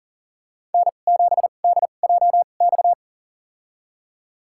Morse code, heard as a keyed tone, about 38 wpm, spaced as for 28 wpm, 700 Hz. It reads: N7DJX
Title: USA Call Signs - 5 Characters - 2x Character Spacing - 40wpm